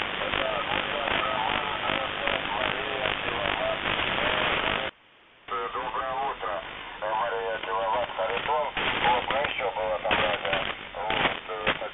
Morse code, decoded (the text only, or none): none